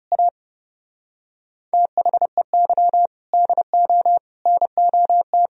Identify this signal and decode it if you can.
Morse code, heard as a keyed tone, about 30 wpm, 700 Hz. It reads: A THEY DO NOT
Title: Speed-Racing - Sets of 5 Words 30wpm